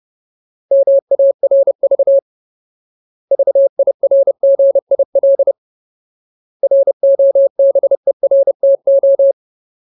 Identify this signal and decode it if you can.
Morse code, heard as a keyed tone, about 30 wpm, 550 Hz. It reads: MARV VIRGIL ROBERTO
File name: Ultimate Word List / 30WPM Ultimate